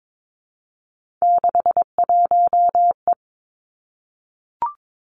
Morse code, heard as a keyed tone, about 22 wpm, 700 Hz. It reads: 61E
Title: Three Challenging Letter-Number Combinations 22wpm